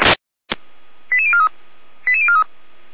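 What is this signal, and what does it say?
Five-tone selective call (ZVEI), heard as quick runs of five tones, tones of about 70 ms. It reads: 88043 88043
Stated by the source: ZVEI five-tone selective call received by SDR, fire brigade alerting.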